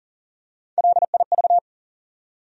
Morse code, heard as a keyed tone, about 40 wpm, 700 Hz. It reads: LIV
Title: USA Common Names - Max 4 Letters - 40wpm